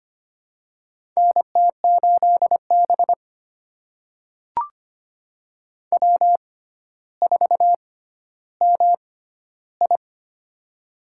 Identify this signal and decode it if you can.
Morse code, heard as a keyed tone, about 25 wpm, spaced as for 13 wpm, 700 Hz. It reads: NT8B W4MI
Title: USA Call Signs - 4 Characters - 6x Character Spacing - 25wpm